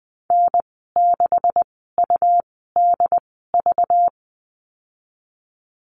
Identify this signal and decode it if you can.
Morse code, heard as a keyed tone, about 20 wpm, spaced as for 15 wpm, 700 Hz. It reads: N6UDV
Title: 238 Lesson 38g - Letter q - Call Signs 20wpm